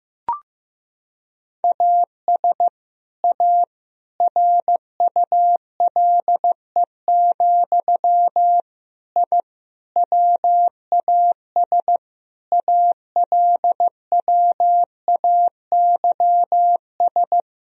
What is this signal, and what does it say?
Morse code, heard as a keyed tone, about 15 wpm, 700 Hz. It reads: AS A RULE, I WAS ALWAYS
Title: Sentences from Top 1000 Words - Review Part 2 - 15wpm